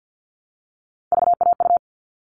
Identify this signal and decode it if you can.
Morse code, heard as a keyed tone, about 48 wpm, 700 Hz. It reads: VAU
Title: All Cut Numbers - 3-Digit 50wpm